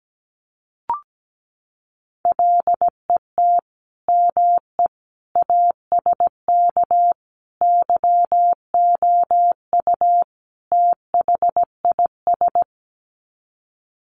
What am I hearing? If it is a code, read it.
Morse code, heard as a keyed tone, about 17 wpm, 700 Hz. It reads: LET ME ASK YOU THIS
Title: Sets of 5 Words 17wpm